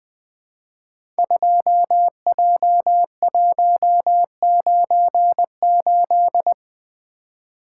Morse code, heard as a keyed tone, about 20 wpm, 700 Hz. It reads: 2J198
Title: Code Groups - 5 Letter:Numbers - 20wpm